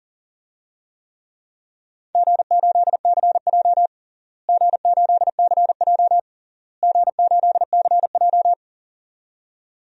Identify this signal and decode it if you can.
Morse code, heard as a keyed tone, about 40 wpm, 700 Hz. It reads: G8CJ G8CJ G8CJ
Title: DXCC-100 Call Signs - 4 Characters - Repeated 3x - 40wpm